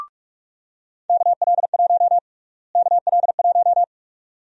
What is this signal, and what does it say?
Morse code, heard as a keyed tone, about 46 wpm, 690 Hz. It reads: KL1 KL1
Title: USA Call Sign Prefixes - Repeated 2x - 45wpm